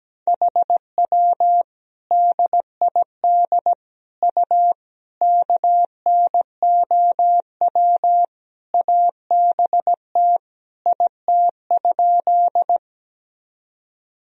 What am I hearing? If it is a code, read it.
Morse code, heard as a keyed tone, about 17 wpm, 700 Hz. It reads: HW DID U KNOW ABT IT?
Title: Sentences from Top 100 Words and Abbreviations 17wpm